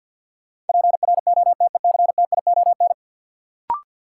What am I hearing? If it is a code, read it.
Morse code, heard as a keyed tone, about 53 wpm, 700 Hz. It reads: PROTECTION
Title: Sentences from Top 1600 Words - New Word Drill - 50wpm